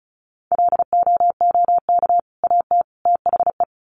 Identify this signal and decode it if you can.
Morse code, heard as a keyed tone, about 34 wpm, 700 Hz. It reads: LOOK AT THE
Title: Sets of 3 Words 35wpm